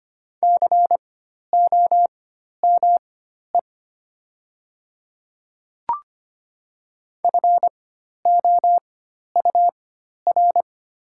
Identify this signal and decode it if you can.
Morse code, heard as a keyed tone, about 25 wpm, spaced as for 12 wpm, 700 Hz. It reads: COME FOUR
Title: 4x Character Spacing - Top 300 Words - 25wpm